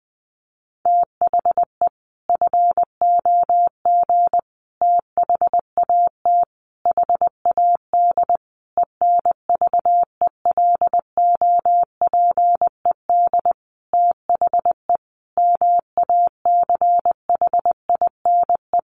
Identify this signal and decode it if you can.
Morse code, heard as a keyed tone, about 20 wpm, 700 Hz. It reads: THE FOG THAT HAD ENVELOPED THE MACHINE
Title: A Sound Of Thunder 20wpm